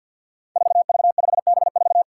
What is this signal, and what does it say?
Morse code, heard as a keyed tone, about 53 wpm, 700 Hz. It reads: 4V5B4
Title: Five Challenging Letter-Number Combinations 50wpm